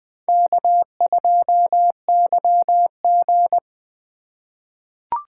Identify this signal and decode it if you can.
Morse code, heard as a keyed tone, about 20 wpm, 700 Hz. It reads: K2YG
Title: USA Call Signs - 4 Characters - 2x Character Spacing - 20wpm